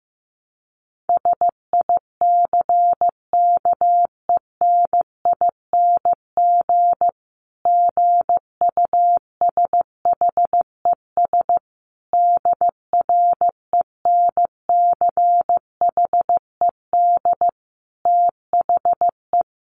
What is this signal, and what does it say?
Morse code, heard as a keyed tone, about 15 wpm, 700 Hz. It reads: SICKENING GUSHES DRENCHED THE
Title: A Sound Of Thunder 15wpm